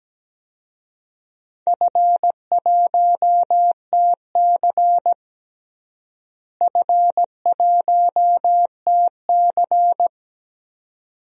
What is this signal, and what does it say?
Morse code, heard as a keyed tone, about 17 wpm, 700 Hz. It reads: F1TC F1TC
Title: DXCC-100 Call Signs - 4 Characters - Repeated 2x - 17wpm